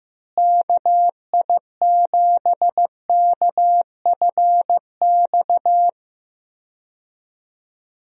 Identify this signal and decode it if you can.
Morse code, heard as a keyed tone, about 15 wpm, 700 Hz. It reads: KI7KFX
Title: USA Call Signs - 6 Characters - Repeated 3x - 15wpm